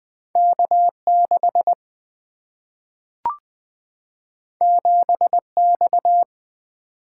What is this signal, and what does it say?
Morse code, heard as a keyed tone, about 20 wpm, 700 Hz. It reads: K6 7X
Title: Single-Letter Single-Number 20wpm